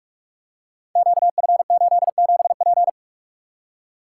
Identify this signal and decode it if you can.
Morse code, heard as a keyed tone, about 44 wpm, 700 Hz. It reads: QF87P
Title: Code Groups - 5 Letter:Numbers - 45wpm